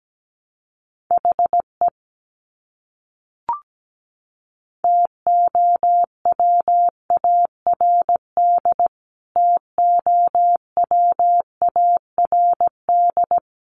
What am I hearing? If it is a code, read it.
Morse code, heard as a keyed tone, about 17 wpm, 700 Hz. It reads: HE TOWARD TOWARD
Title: Top 500 Words Repeated 17wpm